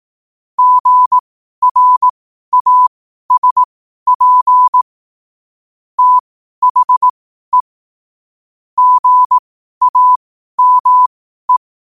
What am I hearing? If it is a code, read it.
Morse code, heard as a keyed tone, about 18 wpm, spaced as for 12 wpm, 1000 Hz. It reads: GRASP THE GAME
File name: qotdfiles / 12wpm